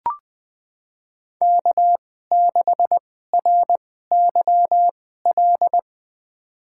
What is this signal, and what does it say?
Morse code, heard as a keyed tone, about 20 wpm, spaced as for 14 wpm, 700 Hz. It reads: K6RYL
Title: USA Call Signs - 5 Characters - 2x Character Spacing - 20wpm